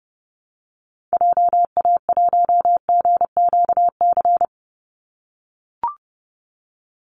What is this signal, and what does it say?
Morse code, heard as a keyed tone, about 30 wpm, 700 Hz. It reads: JA1GQC
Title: Call Signs - 30wpm